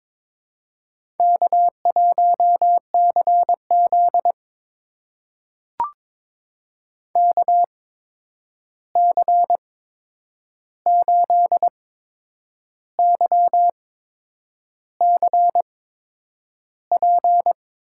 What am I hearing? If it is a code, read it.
Morse code, heard as a keyed tone, about 22 wpm, spaced as for 8 wpm, 700 Hz. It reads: K1CZ KC8YCP
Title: USA Call Signs - 8x Character Spacing - 22wpm